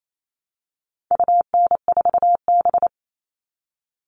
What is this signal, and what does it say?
Morse code, heard as a keyed tone, about 28 wpm, 700 Hz. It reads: UN4B
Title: All Cut Numbers - 4-Digit 28wpm